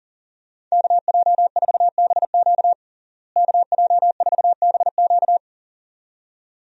Morse code, heard as a keyed tone, about 40 wpm, 700 Hz. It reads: KJ4BQ KJ4BQ
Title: USA Call Signs - 5 Characters - Repeated 2x - 40wpm